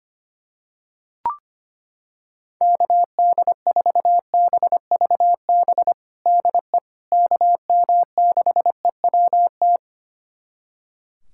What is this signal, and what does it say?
Morse code, heard as a keyed tone, about 25 wpm, 700 Hz. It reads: KD4BVB DE KM6EWT
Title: USA Callsign DE Callsign - 6 Characters - 25wpm